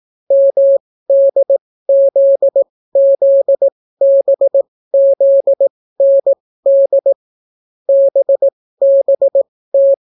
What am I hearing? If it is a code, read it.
Morse code, heard as a keyed tone, about 18 wpm, spaced as for 15 wpm, 550 Hz. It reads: MDZZBZND BBT